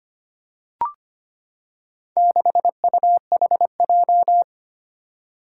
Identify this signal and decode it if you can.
Morse code, heard as a keyed tone, about 25 wpm, 700 Hz. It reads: 6UHJ
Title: Four Challenging Letter-Number Combinations 25wpm